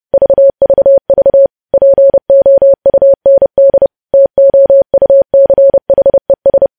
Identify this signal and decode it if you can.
Morse code, heard as a keyed tone, about 30 wpm, 550 Hz. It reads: VVV POUND TOUCHES
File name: podcast / News headlines-1x-30x30